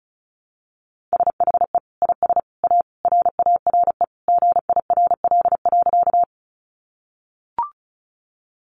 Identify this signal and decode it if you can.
Morse code, heard as a keyed tone, about 34 wpm, 700 Hz. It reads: SHE IS A RARE GIRL.
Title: Sentences from Top 1100 Words 35wpm